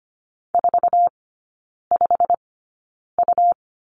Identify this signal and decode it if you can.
Morse code, heard as a keyed tone, about 25 wpm, spaced as for 9 wpm, 700 Hz. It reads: 45U
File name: Lesson 17e - Letter u - Three Characters 25wpm